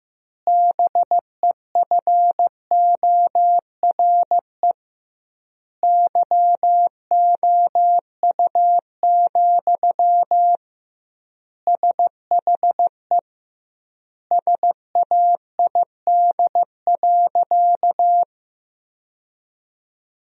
Morse code, heard as a keyed tone, about 15 wpm, 700 Hz. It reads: BEFORE YOU, SHE SAID.
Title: 2x Word-Spacing - Sentences from Top 400 Words - 15wpm